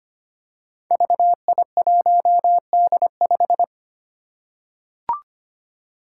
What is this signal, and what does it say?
Morse code, heard as a keyed tone, about 25 wpm, 700 Hz. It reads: VI1D5
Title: Five Challenging Letter-Number Combinations 25wpm